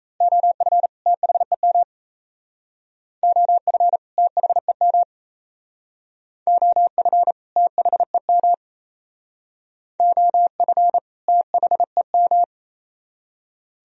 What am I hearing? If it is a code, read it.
Morse code, heard as a keyed tone, about 36 wpm, 690 Hz. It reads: OF THEM OF THEM OF THEM OF THEM